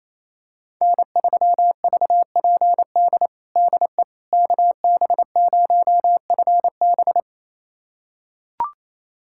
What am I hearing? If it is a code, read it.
Morse code, heard as a keyed tone, about 28 wpm, 700 Hz. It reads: N3VPD DE KB0FB
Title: USA Callsign DE Callsign - 5 Characters - 28wpm